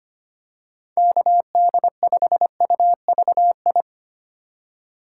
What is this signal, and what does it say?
Morse code, heard as a keyed tone, about 25 wpm, 700 Hz. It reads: KD5UVI
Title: USA Call Signs - 4x Character Spacing - 25wpm